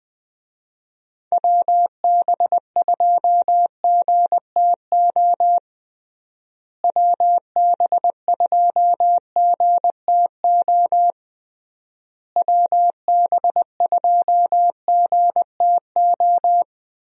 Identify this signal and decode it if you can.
Morse code, heard as a keyed tone, about 20 wpm, 700 Hz. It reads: WB2GTO WB2GTO WB2GTO
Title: USA Call Signs - 6 Characters - Repeated 3x - 20wpm